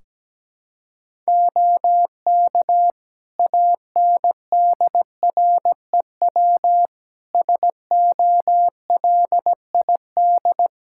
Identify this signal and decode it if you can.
Morse code, heard as a keyed tone, about 17 wpm, 700 Hz. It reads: OK ANDREW SOLID
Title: Practice QSO - Exchange 3 - More Info - 17wpm